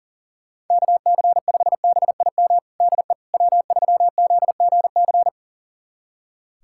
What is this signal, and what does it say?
Morse code, heard as a keyed tone, about 40 wpm, 700 Hz. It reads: KC5BIM DE W3ZGC
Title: USA Callsign DE Callsign 40wpm